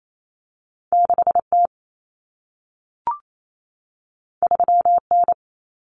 Morse code, heard as a keyed tone, about 28 wpm, 700 Hz. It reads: 6T 3N
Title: Common Cut Numbers - 2-Digit 28wpm